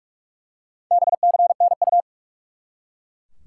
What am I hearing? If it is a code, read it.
Morse code, heard as a keyed tone, about 41 wpm, 700 Hz. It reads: BCNU
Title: Top 90 QSO Elements 45wpm